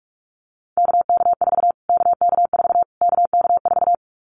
Morse code, heard as a keyed tone, about 44 wpm, 700 Hz. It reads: KK4 KK4 KK4
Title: Callsign Prefixes for Top 25 DXCC - Repeated 3x - 45wpm